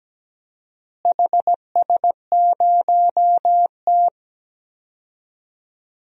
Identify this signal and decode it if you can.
Morse code, heard as a keyed tone, about 17 wpm, 700 Hz. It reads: HS0T
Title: DXCC-100 Call Signs - 4 Characters - Repeated 3x - 17wpm